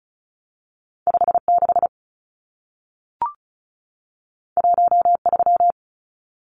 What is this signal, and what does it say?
Morse code, heard as a keyed tone, about 36 wpm, 700 Hz. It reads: 56 13
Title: Numbers - 2-Digit 35wpm